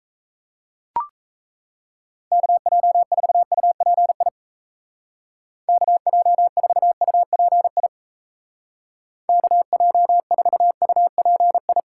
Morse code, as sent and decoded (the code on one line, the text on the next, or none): -.- .--- ....- ..- .--. .. / -.- .--- ....- ..- .--. .. / -.- .--- ....- ..- .--. ..
KJ4UPI KJ4UPI KJ4UPI